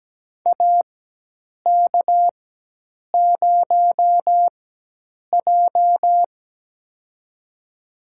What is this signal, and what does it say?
Morse code, heard as a keyed tone, about 17 wpm, spaced as for 8 wpm, 700 Hz. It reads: AK0J
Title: USA Call Signs - 4 Characters - 4x Character Spacing - 17wpm